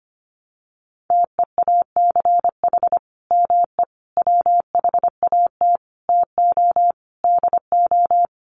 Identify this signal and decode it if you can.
Morse code, heard as a keyed tone, about 25 wpm, 700 Hz. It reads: TEACH ME WHAT TO DO
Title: Sentences from Top 500 Words 25wpm